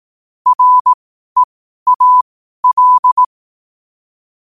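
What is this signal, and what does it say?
Morse code, heard as a keyed tone, about 18 wpm, spaced as for 12 wpm, 1000 Hz. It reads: REAL
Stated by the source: Morse code at 12 WPM.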